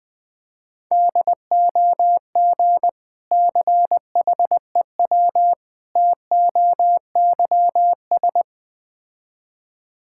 Morse code, heard as a keyed tone, about 20 wpm, 700 Hz. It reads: DOG CHEW TOYS
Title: Difficult Words with Hints - 20wpm